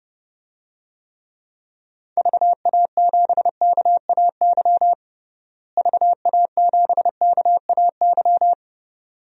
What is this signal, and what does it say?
Morse code, heard as a keyed tone, about 32 wpm, 700 Hz. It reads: VA7KAY VA7KAY
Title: DXCC-100 Call Signs - 6 Characters - Repeated 2x - 30wpm